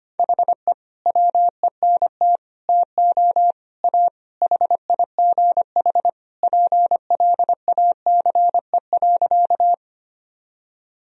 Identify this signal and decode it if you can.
Morse code, heard as a keyed tone, about 25 wpm, 700 Hz. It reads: HE WENT TO A HIGH PLACE.
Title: Sentences from Top 1000 Words - Review Part 1 - 25wpm